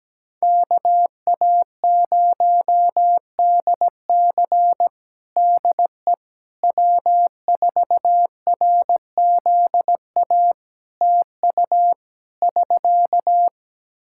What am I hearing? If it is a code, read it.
Morse code, heard as a keyed tone, about 17 wpm, 700 Hz. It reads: KA0DC DE W4RZA TU <SK>
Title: Practice QSO - Exchange 4 - Ending - 17wpm